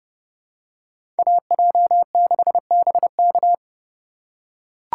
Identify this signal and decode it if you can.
Morse code, heard as a keyed tone, about 29 wpm, 700 Hz. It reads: AJ6BK